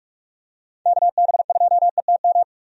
Morse code, heard as a keyed tone, about 43 wpm, 700 Hz. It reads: KB2ETM
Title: USA Call Signs - 6 Characters - 45wpm